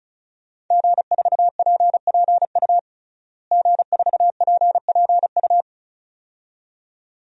Morse code, heard as a keyed tone, about 36 wpm, 700 Hz. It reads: G4PPU G4PPU
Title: DXCC-100 Call Signs - 5 Characters - Repeated 2x - 35wpm